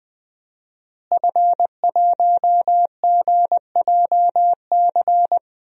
Morse code, heard as a keyed tone, about 20 wpm, 700 Hz. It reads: F1GJC
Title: DXCC-100 Call Signs - 5 Characters - Repeated 3x - 20wpm